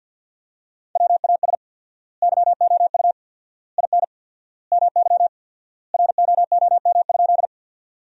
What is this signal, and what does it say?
Morse code, heard as a keyed tone, about 50 wpm, 700 Hz. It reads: WAS YOU IN MY ROOM?
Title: 4x Word-Spacing - Sentences from Top 300 Words - 50wpm